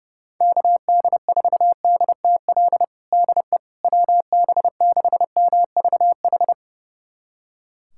Morse code, heard as a keyed tone, about 30 wpm, 700 Hz. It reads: KD4DTL DE WB6MVH
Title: USA Callsign DE Callsign - 6 Characters - 30wpm